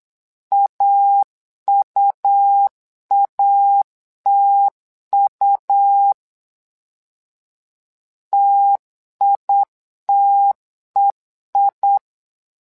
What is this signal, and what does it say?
Morse code, heard as a keyed tone, about 8 wpm, 800 Hz. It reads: AUATU TITEI
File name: LEZIONI / 2 UHDN / 2 U 8wpm 2028